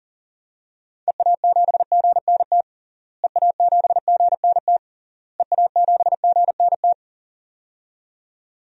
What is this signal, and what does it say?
Morse code, heard as a keyed tone, about 40 wpm, 700 Hz. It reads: EA7GNT EA7GNT EA7GNT
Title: DXCC-100 Call Signs - 6 Characters - Repeated 3x - 40wpm